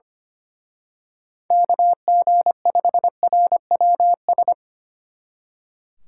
Morse code, heard as a keyed tone, about 28 wpm, 700 Hz. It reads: KG5RWS
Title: Speed Racing - USA 2x3 Call Signs 25wpm